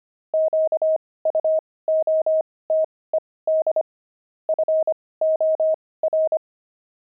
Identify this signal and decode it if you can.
Morse code, heard as a keyed tone, about 25 wpm, spaced as for 18 wpm, 630 Hz. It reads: QUOTED FOR
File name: pangram-fs-18-cs-25-low